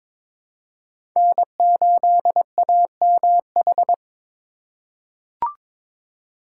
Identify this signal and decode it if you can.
Morse code, heard as a keyed tone, about 21 wpm, 700 Hz. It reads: N8AMH
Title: USA Call Signs - 5 Characters - 8x Character Spacing - 22wpm